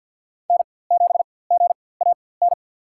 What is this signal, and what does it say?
Morse code, heard as a keyed tone, about 52 wpm, spaced as for 24 wpm, 700 Hz. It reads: N7GAN
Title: USA Call Signs - 4x Character Spacing - 50wpm